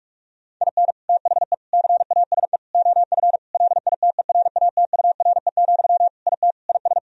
Morse code, heard as a keyed tone, about 46 wpm, 700 Hz. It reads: IN THE CASE OF LITERATURE, IT IS
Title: Sentences from Top 1000 Words - Review Part 3 - 45wpm